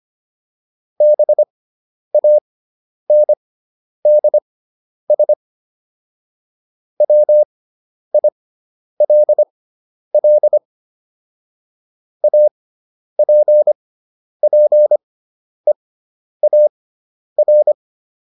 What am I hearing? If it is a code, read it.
Morse code, heard as a keyed tone, about 25 wpm, spaced as for 10 wpm, 600 Hz. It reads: BANDS WILL APPEAR